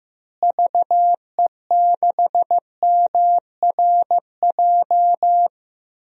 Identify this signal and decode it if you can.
Morse code, heard as a keyed tone, about 16 wpm, 700 Hz. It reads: VE6MRJ
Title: DXCC-100 Call Signs - 6 Characters - Repeated 2x - 15wpm